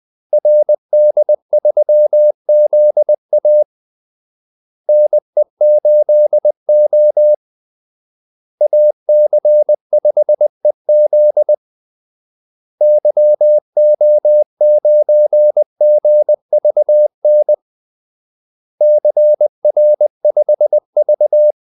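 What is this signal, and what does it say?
Morse code, heard as a keyed tone, about 20 wpm, 600 Hz. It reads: RD3ZA NE8O AC5EZ YO9GVN CR5V
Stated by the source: Morse call-sign practice at 20 wpm.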